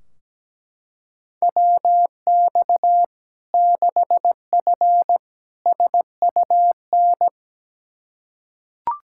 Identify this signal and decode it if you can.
Morse code, heard as a keyed tone, about 17 wpm, 700 Hz. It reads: WX 6F SUN
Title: Practice QSO - Exchange 2 - Info - 17wpm